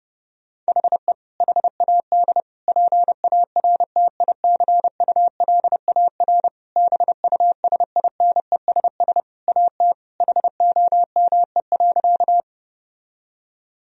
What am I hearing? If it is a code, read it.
Morse code, heard as a keyed tone, about 30 wpm, 700 Hz. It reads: HE HAD PARTICULAR BUSINESS AT HOME.
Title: Sentences from Top 1000 Words - Review Part 4 - 30wpm